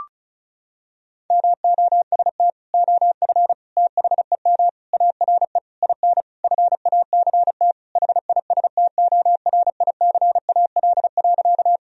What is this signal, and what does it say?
Morse code, heard as a keyed tone, about 36 wpm, 700 Hz. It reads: MOST OF THEM ARE IN FACT HISTORICAL.
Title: Sentences from Top 1000 Words - Review Part 4 - 35wpm